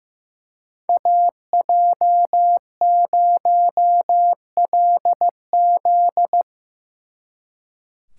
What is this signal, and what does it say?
Morse code, heard as a keyed tone, about 17 wpm, 700 Hz. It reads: AJ0LZ
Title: Speed Racing - USA 2x2 Call Signs 15wpm